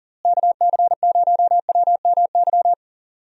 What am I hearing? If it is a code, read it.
Morse code, heard as a keyed tone, about 40 wpm, 700 Hz. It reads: KC0WMY
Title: USA Call Signs - 6 Characters - Repeated 2x - 40wpm